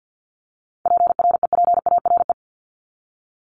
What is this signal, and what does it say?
Morse code, heard as a keyed tone, about 50 wpm, 700 Hz. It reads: PREPARE
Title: ED Suffix - Top 100 Words with Base Words 50wpm